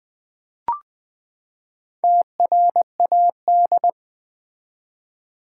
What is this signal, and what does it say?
Morse code, heard as a keyed tone, about 20 wpm, 700 Hz. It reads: TRAD
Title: Build A Word - 4 Letters At A Time - 11-Letter Words - 20wpm